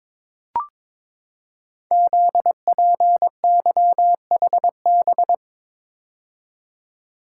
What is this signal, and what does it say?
Morse code, heard as a keyed tone, about 22 wpm, 700 Hz. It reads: ZPYHB
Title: Code Groups - 5 Letter:Numbers - 22wpm